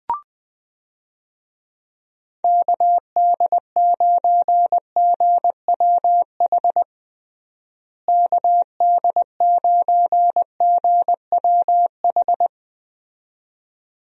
Morse code, heard as a keyed tone, about 20 wpm, 700 Hz. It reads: KD9GWH KD9GWH
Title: USA Call Signs - 6 Characters - Repeated 2x - 20wpm